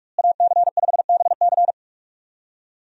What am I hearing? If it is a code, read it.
Morse code, heard as a keyed tone, about 46 wpm, 700 Hz. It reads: AC5BC
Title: USA Call Signs - 5 Characters - Repeated 2x - 45wpm